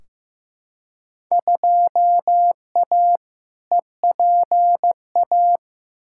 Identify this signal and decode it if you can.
Morse code, heard as a keyed tone, about 16 wpm, 700 Hz. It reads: 2A EPA
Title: Field Day Exchanges - 15wpm